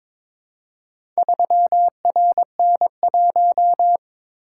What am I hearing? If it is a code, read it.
Morse code, heard as a keyed tone, about 22 wpm, 700 Hz. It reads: 3RN1
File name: practice / 22wpm / Four Letter-Number Combinations - 22wpm